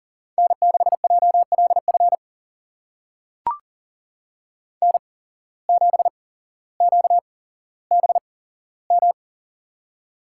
Morse code, heard as a keyed tone, about 40 wpm, spaced as for 17 wpm, 700 Hz. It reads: N6JLF N7QBM
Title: USA Call Signs - 5 Characters - 8x Character Spacing - 40wpm